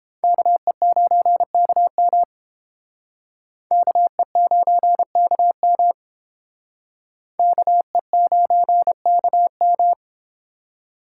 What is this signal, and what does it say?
Morse code, heard as a keyed tone, about 29 wpm, 700 Hz. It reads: KE9KM KE9KM KE9KM